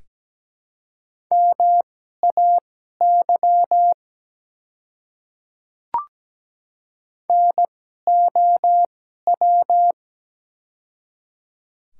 Morse code, heard as a keyed tone, about 17 wpm, spaced as for 12 wpm, 700 Hz. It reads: MAY NOW
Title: 2x Character Spacing - Top 100 Words - 17wpm